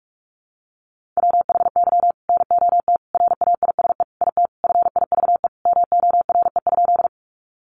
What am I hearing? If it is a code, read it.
Morse code, heard as a keyed tone, about 44 wpm, 700 Hz. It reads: WHY NOT RAISE IT FIVE MORE?
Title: Sentences from Top 1400 Words 45wpm